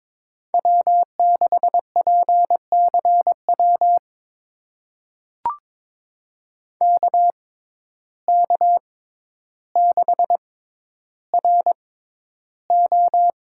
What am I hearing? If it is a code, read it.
Morse code, heard as a keyed tone, about 22 wpm, spaced as for 11 wpm, 700 Hz. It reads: W6PCW KK6RO
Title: USA Call Signs - 5 Characters - 6x Character Spacing - 22wpm